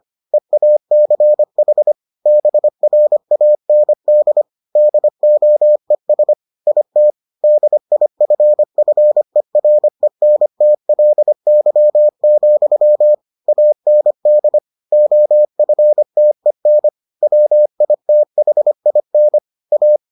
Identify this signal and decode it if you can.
Morse code, heard as a keyed tone, about 25 wpm, 600 Hz. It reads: EACH BRAND DOES IT DIFFERENTLY, AND OFTEN WITHIN A